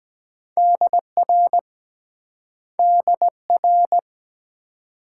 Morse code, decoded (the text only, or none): DR DR